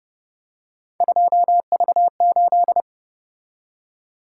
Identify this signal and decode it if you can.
Morse code, heard as a keyed tone, about 29 wpm, 700 Hz. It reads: 2V8